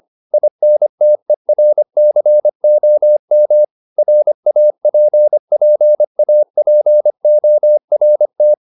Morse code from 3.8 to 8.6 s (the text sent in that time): RAPPAPORT